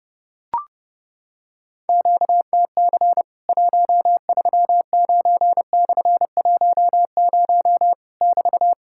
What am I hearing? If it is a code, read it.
Morse code, heard as a keyed tone, about 30 wpm, 700 Hz. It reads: QTC 139/10 <BT>
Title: Worked All Europe DX Contest - QTC Exchange - 30wpm